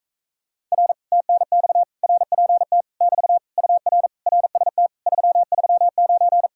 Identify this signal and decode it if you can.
Morse code, heard as a keyed tone, about 44 wpm, 700 Hz. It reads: R TNX RPT <BT> UR RST 339